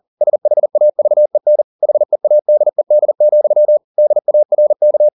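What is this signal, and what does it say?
Morse code, heard as a keyed tone, about 40 wpm, 600 Hz. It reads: SHAVEN HEADED, DARK
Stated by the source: Morse code at 40 wpm, practice text, CW med Gnister.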